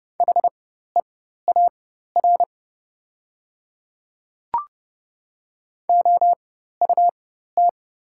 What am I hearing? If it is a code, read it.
Morse code, heard as a keyed tone, about 29 wpm, spaced as for 14 wpm, 700 Hz. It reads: HEAR OUT
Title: 4x Character Spacing - Top 300 Words - 30wpm